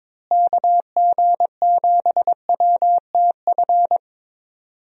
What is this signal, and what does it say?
Morse code, heard as a keyed tone, about 22 wpm, 700 Hz. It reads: KG7WTF
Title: USA Call Signs - 6 Characters - Repeated 2x - 22wpm